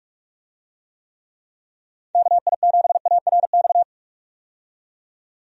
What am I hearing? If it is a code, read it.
Morse code, heard as a keyed tone, about 42 wpm, 700 Hz. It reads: KI7ARX
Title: USA Call Signs - 6 Characters - 45wpm